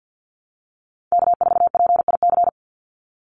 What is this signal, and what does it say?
Morse code, heard as a keyed tone, about 50 wpm, 700 Hz. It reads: K4PIC
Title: NA SOTA Chaser Callsigns 50wpm